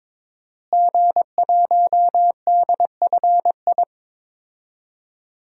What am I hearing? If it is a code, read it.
Morse code, heard as a keyed tone, about 21 wpm, 700 Hz. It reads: G1DFI
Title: DXCC-100 Call Signs - 5 Characters - Repeated 3x - 22wpm